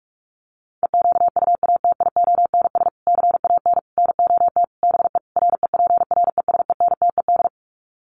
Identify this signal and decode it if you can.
Morse code, heard as a keyed tone, about 44 wpm, 700 Hz. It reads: EQUATIONS CAN NOT BE REPRESENTED
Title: Sets of 5 Words 45wpm